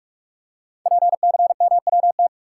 Morse code, heard as a keyed tone, about 44 wpm, 700 Hz. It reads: PCMWT
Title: Code Groups - 5 Letters - 45wpm